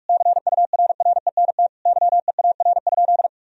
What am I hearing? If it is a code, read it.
Morse code, heard as a keyed tone, about 41 wpm, 700 Hz. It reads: CURRENT YEAR?
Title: Sentences from Top 1000 Words - Review Part 4 - 45wpm